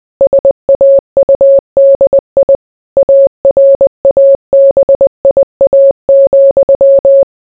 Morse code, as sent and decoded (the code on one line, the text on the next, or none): ... .- ..- -.. .. / .- .-. .- -... .. .- --..--
SAUDI ARABIA,